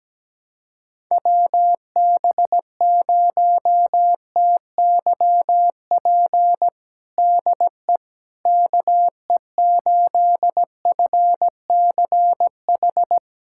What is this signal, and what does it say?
Morse code, heard as a keyed tone, about 17 wpm, 700 Hz. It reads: WB0TYP DE KE8FCH
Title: USA Callsign DE Callsign - 6 Characters - 17wpm